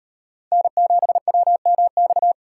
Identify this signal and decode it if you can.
Morse code, heard as a keyed tone, about 38 wpm, 700 Hz. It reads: N7WMX